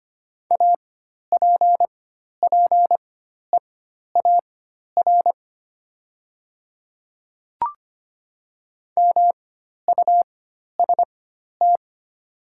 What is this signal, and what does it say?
Morse code, heard as a keyed tone, about 25 wpm, spaced as for 12 wpm, 700 Hz. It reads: APPEAR MUST
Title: 4x Character Spacing - Top 500 Words - 25wpm